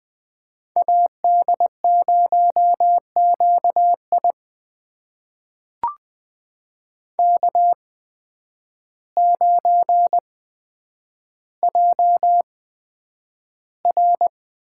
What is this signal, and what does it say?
Morse code, heard as a keyed tone, about 20 wpm, spaced as for 9 wpm, 700 Hz. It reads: AD0QI K9JR
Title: USA Call Signs - 8x Character Spacing - 20wpm